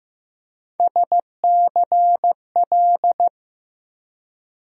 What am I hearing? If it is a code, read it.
Morse code, heard as a keyed tone, about 15 wpm, 700 Hz. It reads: SCL
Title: USA Call Sign Postfixes - 3 Letters 15wpm